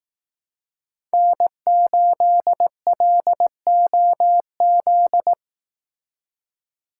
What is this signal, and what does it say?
Morse code, heard as a keyed tone, about 18 wpm, 700 Hz. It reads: N8LOZ
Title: Speed Racing - USA 1x3 Call Signs 15wpm